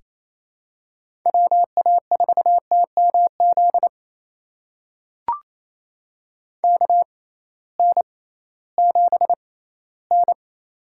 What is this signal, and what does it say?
Morse code, heard as a keyed tone, about 28 wpm, spaced as for 16 wpm, 700 Hz. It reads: WA4TMZ KN7N
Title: USA Call Signs - 6x Character Spacing - 28wpm